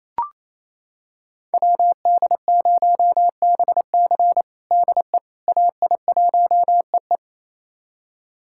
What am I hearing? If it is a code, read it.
Morse code, heard as a keyed tone, about 28 wpm, 700 Hz. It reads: WD0BC DE AI1EE
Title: USA Callsign DE Callsign - 5 Characters - 28wpm